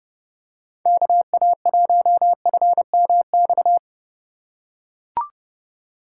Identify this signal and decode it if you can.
Morse code, heard as a keyed tone, about 30 wpm, 700 Hz. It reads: KA1FMX